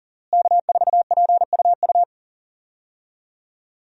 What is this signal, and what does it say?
Morse code, heard as a keyed tone, about 39 wpm, 700 Hz. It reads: K4PUU
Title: USA Call Signs - 5 Characters - Repeated 3x - 40wpm